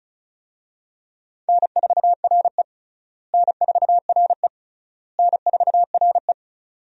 Morse code, heard as a keyed tone, about 36 wpm, 700 Hz. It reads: N4RE N4RE N4RE
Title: USA Call Signs - 4 Characters - Repeated 3x - 35wpm